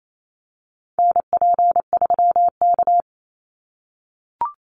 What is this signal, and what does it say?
Morse code, heard as a keyed tone, about 28 wpm, 700 Hz. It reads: NP3K
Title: Call Signs - 28wpm